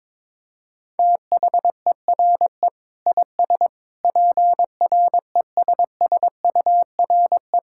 Morse code, heard as a keyed tone, about 22 wpm, 700 Hz. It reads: THERE IS PRESSURE
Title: Sentences from Top 1000 Words - Review Part 6 - 22wpm